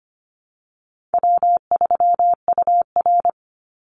Morse code, heard as a keyed tone, about 25 wpm, 700 Hz. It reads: W3UR
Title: Call Signs Repeated 3x - 25wpm